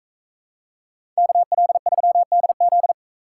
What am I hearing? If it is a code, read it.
Morse code, heard as a keyed tone, about 41 wpm, 700 Hz. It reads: KL3DZ